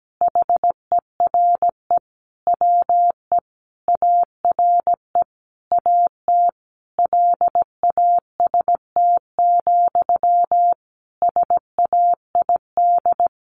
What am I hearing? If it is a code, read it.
Morse code, heard as a keyed tone, about 17 wpm, 700 Hz. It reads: HERE WE ARE AT LAST, SAID
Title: Sentences from Top 300 Words 17wpm